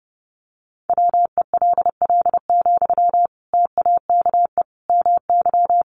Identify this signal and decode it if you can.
Morse code, heard as a keyed tone, about 30 wpm, 700 Hz. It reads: WELL, TAKE MY
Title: Sentences from Top 1100 Words 30wpm